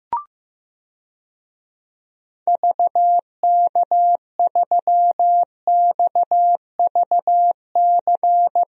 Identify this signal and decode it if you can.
Morse code, heard as a keyed tone, about 16 wpm, 700 Hz. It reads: VK3XVC
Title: DXCC-100 Call Signs - 6 Characters - Repeated 3x - 15wpm